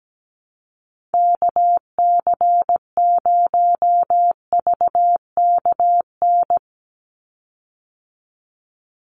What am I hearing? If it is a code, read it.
Morse code, heard as a keyed tone, about 17 wpm, 700 Hz. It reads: KC0VKN
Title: Call Signs Repeated 2x - 17wpm